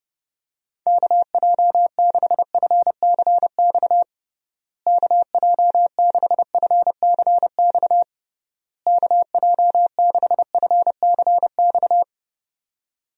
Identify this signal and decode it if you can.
Morse code, heard as a keyed tone, about 32 wpm, 700 Hz. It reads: KJ6FCX KJ6FCX KJ6FCX
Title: USA Call Signs - 6 Characters - Repeated 3x - 30wpm